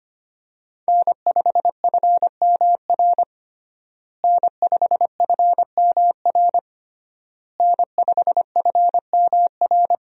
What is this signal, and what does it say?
Morse code, heard as a keyed tone, about 25 wpm, 700 Hz. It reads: N5FMR N5FMR N5FMR
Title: USA Call Signs - 5 Characters - Repeated 3x - 25wpm